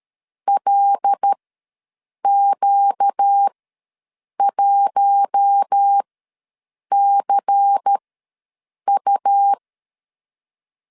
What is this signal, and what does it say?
Morse code, heard as a keyed tone, about 13 wpm, spaced as for 7 wpm, 800 Hz. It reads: LQ1CU
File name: LEZIONI / 6 PKX0 / 6 PKX0 ws 12wpm 3448